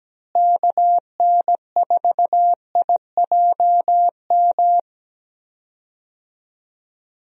Morse code, heard as a keyed tone, about 17 wpm, 700 Hz. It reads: KN4IJM
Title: USA Call Signs - 6 Characters - Repeated 2x - 17wpm